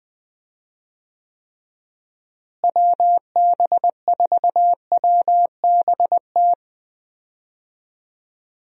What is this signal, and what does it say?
Morse code, heard as a keyed tone, about 20 wpm, 700 Hz. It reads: WB4WBT
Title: USA Call Signs - 6 Characters - Repeated 3x - 20wpm